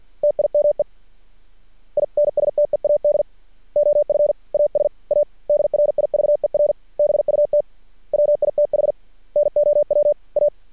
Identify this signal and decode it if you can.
Morse code, heard as a keyed tone, about 50 wpm, 590 Hz. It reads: TIME INSTEAD OF AS A DRIVER BUT WITH NOW A
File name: archived / CRSnet